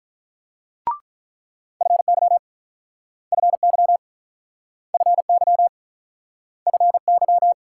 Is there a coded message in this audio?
Yes: Morse code from 1.8 to 7.5 s, heard as a keyed tone, about 48 wpm, 700 Hz. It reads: FY FY FY FY